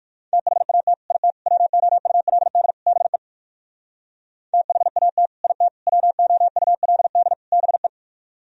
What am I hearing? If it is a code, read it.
Morse code, heard as a keyed tone, about 57 wpm, 700 Hz. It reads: THAT IT WOULD BE THAT IT WOULD BE